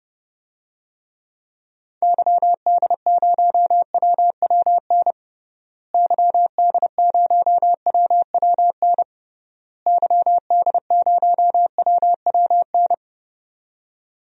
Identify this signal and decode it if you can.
Morse code, heard as a keyed tone, about 29 wpm, 700 Hz. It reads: YD0WWN YD0WWN YD0WWN
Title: DXCC-100 Call Signs - 6 Characters - Repeated 3x - 30wpm